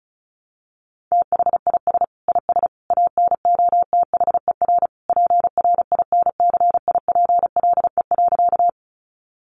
Morse code, heard as a keyed tone, about 34 wpm, 700 Hz. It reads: THIS IS ANOTHER PRINCIPLE.
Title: Sentences from Top 1400 Words 35wpm